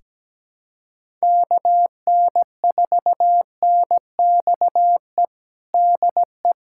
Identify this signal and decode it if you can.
Morse code, heard as a keyed tone, about 17 wpm, 700 Hz. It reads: KN4NXE DE
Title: USA Callsign DE Callsign 17wpm